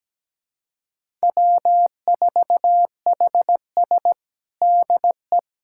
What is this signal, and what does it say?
Morse code, heard as a keyed tone, about 17 wpm, 700 Hz. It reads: W4HS DE
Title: USA Callsign DE Callsign - 4 Characters - 17wpm